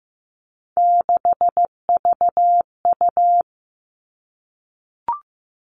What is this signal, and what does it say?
Morse code, heard as a keyed tone, about 15 wpm, 700 Hz. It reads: 6VU
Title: All Cut Numbers - 3-Digit 15wpm